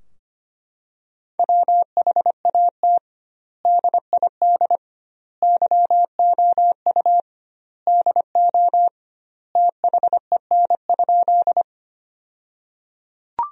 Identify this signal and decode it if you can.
Morse code, heard as a keyed tone, about 25 wpm, 700 Hz. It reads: WHAT DID YOU DO THEN?
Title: 2x Word-Spacing - Sentences from Top 100 Words - 25wpm